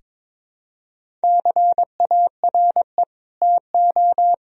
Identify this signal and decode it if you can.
Morse code, heard as a keyed tone, about 22 wpm, 700 Hz. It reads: CARE TO
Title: Sentences from Top 1000 Words - Review Part 1 - 22wpm